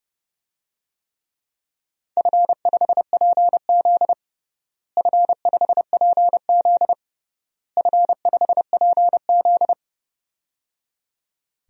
Morse code, heard as a keyed tone, about 30 wpm, 700 Hz. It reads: F5PZ F5PZ F5PZ
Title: DXCC-100 Call Signs - 4 Characters - Repeated 3x - 30wpm